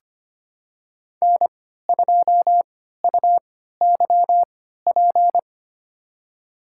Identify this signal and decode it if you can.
Morse code, heard as a keyed tone, about 25 wpm, spaced as for 14 wpm, 700 Hz. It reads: N2UYP
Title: USA Call Signs - 5 Characters - 3x Character Spacing - 25wpm